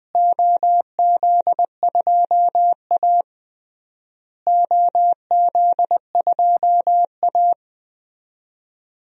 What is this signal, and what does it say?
Morse code, heard as a keyed tone, about 20 wpm, 700 Hz. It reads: OZ2A OZ2A
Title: DXCC-100 Call Signs - 4 Characters - Repeated 3x - 20wpm